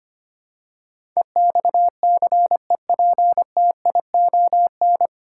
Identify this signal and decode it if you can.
Morse code, heard as a keyed tone, about 25 wpm, 700 Hz. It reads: EXCEPTION
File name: Nine Letter Words - 25wpm